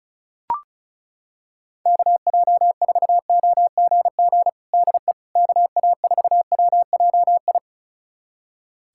Morse code, heard as a keyed tone, about 36 wpm, 700 Hz. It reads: KJ4OGG DE KA4WJI
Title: USA Callsign DE Callsign - 6 Characters - 35wpm